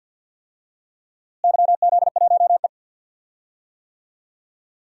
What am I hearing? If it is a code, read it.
Morse code, heard as a keyed tone, about 52 wpm, 700 Hz. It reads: YZ1E
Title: DXCC-100 Call Signs - 4 Characters - Repeated 2x - 50wpm